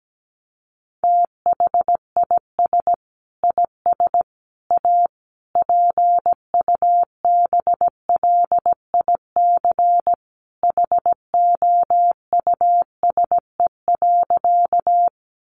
Morse code, heard as a keyed tone, about 17 wpm, 700 Hz. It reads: THIS IS A PUBLIC HOUSE.
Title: Sentences from Top 600 Words 17wpm